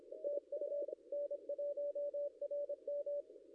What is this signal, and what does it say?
Morse code, heard as a keyed tone, about 25 wpm, 560 Hz. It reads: TU N1RM
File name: contests / ss-cw-k5tr-op-n5rz -0700